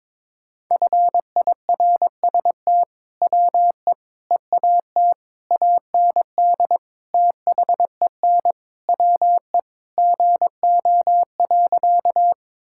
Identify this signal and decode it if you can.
Morse code, heard as a keyed tone, about 22 wpm, 700 Hz. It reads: FIRST WE EAT AND THEN WE GO.
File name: practice / 22wpm / Sentences with 2-Word Phrases - 22wpm